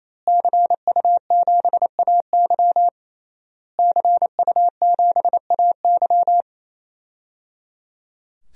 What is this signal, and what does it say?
Morse code, heard as a keyed tone, about 28 wpm, 700 Hz. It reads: CU7AY CU7AY
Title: DXCC-100 Call Signs - 5 Characters - Repeated 3x - 28wpm